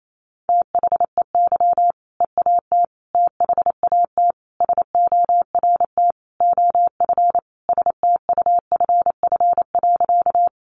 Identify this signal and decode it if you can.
Morse code, heard as a keyed tone, about 28 wpm, 700 Hz. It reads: THEY EAT THAT SORT OF STUFF.
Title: Sentences from Top 1400 Words 28wpm